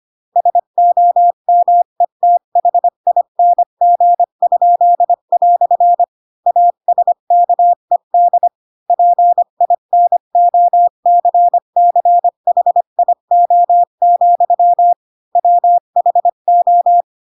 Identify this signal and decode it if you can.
Morse code, heard as a keyed tone, about 25 wpm, 700 Hz. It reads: SOMETHING?" ASKED PINOCCHIO, WHO
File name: Pinocchio / Pinocchio ch4 20 25 ik1hge